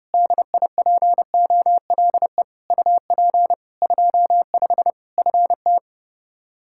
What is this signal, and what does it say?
Morse code, heard as a keyed tone, about 32 wpm, 700 Hz. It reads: DIPOLE UP 25 FT
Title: Practice QSO - Exchange 2 - Info - 30wpm